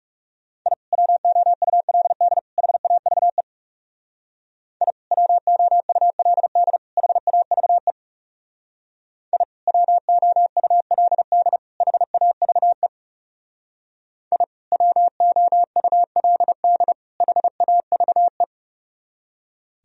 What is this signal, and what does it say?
Morse code, heard as a keyed tone, about 38 wpm, 700 Hz. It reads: I WOULD HAVE I WOULD HAVE I WOULD HAVE I WOULD HAVE
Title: Speed-Racing - Sets of 3 Words 30wpm